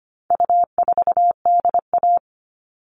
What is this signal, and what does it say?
Morse code, heard as a keyed tone, about 25 wpm, 700 Hz. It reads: U4DA
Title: All Cut Numbers - 4-Digit 25wpm